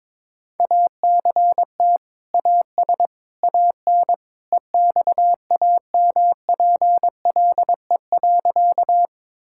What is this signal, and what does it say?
Morse code, heard as a keyed tone, about 22 wpm, 700 Hz. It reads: ACT AS AN EXAMPLE.
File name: practice / 22wpm / Sentences with 2-Word Phrases - 22wpm